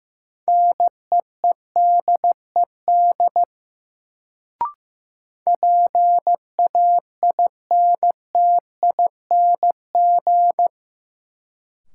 Morse code, heard as a keyed tone, about 15 wpm, 700 Hz. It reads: NEEDED PAINTING
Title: Sentences from Top 1000 Words - Review Part 6 - New Word Drill - 15wpm